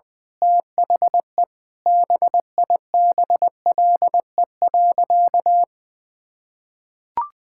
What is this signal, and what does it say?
Morse code, heard as a keyed tone, about 20 wpm, 700 Hz. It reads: THE BIBLE.
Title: Sentences with 9-Word Phrases - 20wpm